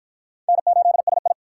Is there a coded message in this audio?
Yes: Morse code from 0.5 to 1.3 s, heard as a keyed tone, about 53 wpm, 700 Hz. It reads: N8SI